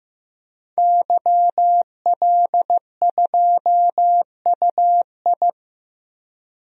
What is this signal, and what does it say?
Morse code, heard as a keyed tone, about 15 wpm, 700 Hz. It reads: YL2UI
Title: DXCC-100 Call Signs - 5 Characters - 15wpm